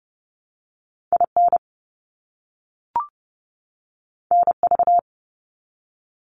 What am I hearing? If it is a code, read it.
Morse code, heard as a keyed tone, about 30 wpm, 700 Hz. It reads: IN NV
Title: US States 30wpm